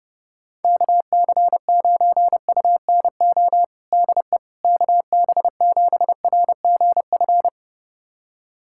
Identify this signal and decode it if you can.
Morse code, heard as a keyed tone, about 30 wpm, 700 Hz. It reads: KC9UNO DE KB7RGF
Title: USA Callsign DE Callsign 30wpm